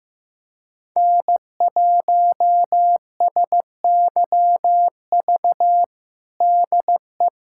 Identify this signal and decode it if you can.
Morse code, heard as a keyed tone, about 15 wpm, 700 Hz. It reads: N1SYV DE
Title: USA Callsign DE Callsign - 5 Characters - 15wpm